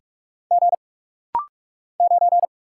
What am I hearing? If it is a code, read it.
Morse code, heard as a keyed tone, about 44 wpm, 700 Hz. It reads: G 9
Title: Single Letter-Number - Rapid-Fire 45wpm